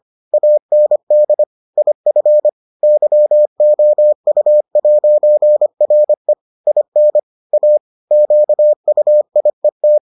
Morse code, heard as a keyed tone, about 25 wpm, 600 Hz. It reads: AND IF YOU'RE IN A QUIET